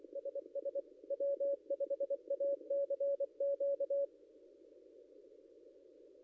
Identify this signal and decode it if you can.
Morse code, heard as a keyed tone, about 24 wpm, 560 Hz. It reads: SS W5ACQ
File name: contests / ss-cw-k5tr-op-n5rz -0800